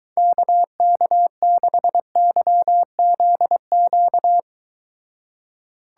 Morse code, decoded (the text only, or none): KK6YZQ